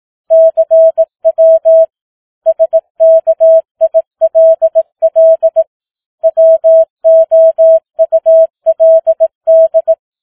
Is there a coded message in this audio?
Yes: Morse code from 0.3 to 9.9 s, heard as a keyed tone, about 18 wpm, 650 Hz. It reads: CW SKILL WOULD